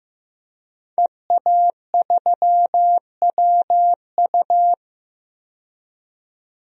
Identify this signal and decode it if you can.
Morse code, heard as a keyed tone, about 15 wpm, 700 Hz. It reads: EA3WU
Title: DXCC-100 Call Signs - 5 Characters - 15wpm